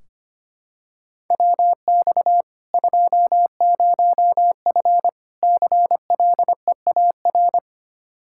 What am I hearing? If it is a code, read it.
Morse code, heard as a keyed tone, about 25 wpm, 700 Hz. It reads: WX 20F CLEAR
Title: K1USN SST QSO Practice 25wpm